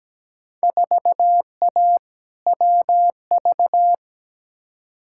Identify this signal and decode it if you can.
Morse code, heard as a keyed tone, about 17 wpm, 700 Hz. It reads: 4A WV
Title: Field Day Exchanges Encore - 17wpm